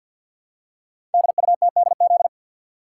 Morse code, heard as a keyed tone, about 52 wpm, 690 Hz. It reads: DUTDZ
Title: Code Groups - 5 Letters - 50wpm